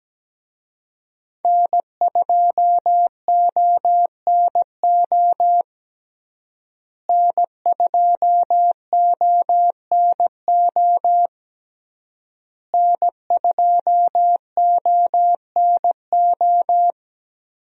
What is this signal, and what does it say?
Morse code, heard as a keyed tone, about 17 wpm, 700 Hz. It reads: N2ONO N2ONO N2ONO
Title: USA Call Signs - 5 Characters - Repeated 3x - 17wpm